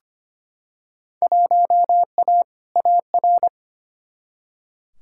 Morse code, heard as a keyed tone, about 25 wpm, 700 Hz. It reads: 1A AR
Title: Field Day Exchanges - 25wpm